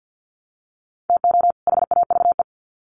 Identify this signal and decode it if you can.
Morse code, heard as a keyed tone, about 50 wpm, 700 Hz. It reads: TO HAVE
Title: Sets of 2 Words 50wpm